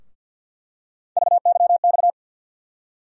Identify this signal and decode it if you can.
Morse code, heard as a keyed tone, about 52 wpm, 700 Hz. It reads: VYX
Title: USA Call Sign Postfixes - 3 Letters 50wpm